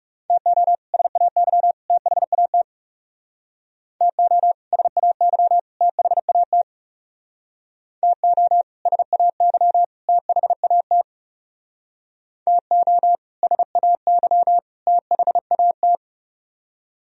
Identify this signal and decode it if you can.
Morse code, heard as a keyed tone, about 36 wpm, 700 Hz. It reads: TO SAY THAT TO SAY THAT TO SAY THAT TO SAY THAT
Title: Speed-Racing - Sets of 3 Words 30wpm